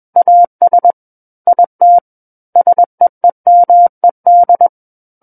Morse code, heard as a keyed tone, about 21 wpm, 700 Hz. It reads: AS IT SEEMED
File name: Book One / 12-Chapter 12 - 21 WPM